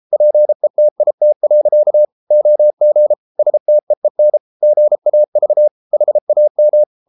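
Morse code, heard as a keyed tone, about 33 wpm, 600 Hz. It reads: PETIT. OG STEEN GAV HAM
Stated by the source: Morse code at 33 wpm, QRQ.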